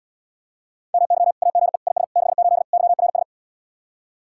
Morse code, heard as a keyed tone, about 79 wpm, 700 Hz. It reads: MY AGE IS 68 YRS